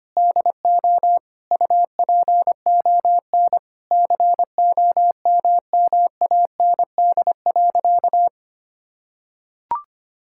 Morse code, heard as a keyed tone, about 25 wpm, 700 Hz. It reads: DO UPON COMMAND.
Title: Sentences from Top 1600 Words 25wpm